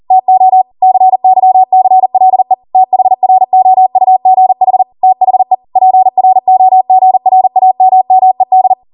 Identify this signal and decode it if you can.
Morse code, heard as a keyed tone, about 40 wpm, 750 Hz. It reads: TO CYCLE THROUGH THE PROGRAMMED
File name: Morse / 40WPM